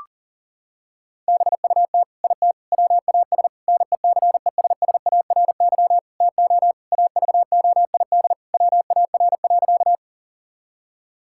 Morse code, heard as a keyed tone, about 40 wpm, 700 Hz. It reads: BUT IT WAS NECESSARY TO AVOID WAR.
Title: Sentences from Top 1000 Words - Review Part 6 - 40wpm